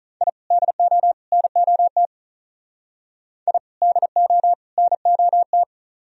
Morse code, heard as a keyed tone, about 44 wpm, 700 Hz. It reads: I DO NOT I DO NOT